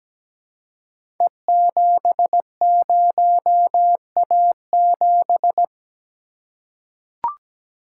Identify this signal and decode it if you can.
Morse code, heard as a keyed tone, about 17 wpm, 700 Hz. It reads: E70A7
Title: Code Groups - 5 Letter:Numbers - 17wpm